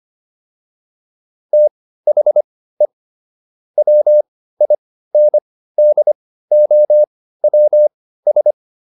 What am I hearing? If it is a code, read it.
Morse code, heard as a keyed tone, about 25 wpm, spaced as for 15 wpm, 600 Hz. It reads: THE WINDOWS